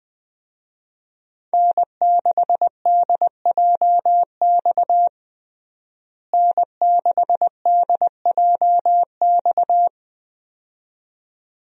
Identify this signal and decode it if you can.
Morse code, heard as a keyed tone, about 20 wpm, 700 Hz. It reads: N6DJX N6DJX
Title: USA Call Signs - 5 Characters - Repeated 2x - 20wpm